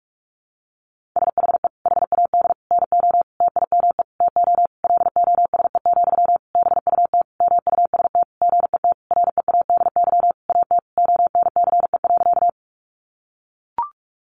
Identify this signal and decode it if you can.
Morse code, heard as a keyed tone, about 44 wpm, 700 Hz. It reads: SHE HAD NO TIME TO LOSE, BUT MUST GET READY AT ONCE.
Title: Sentences from Top 1100 Words 45wpm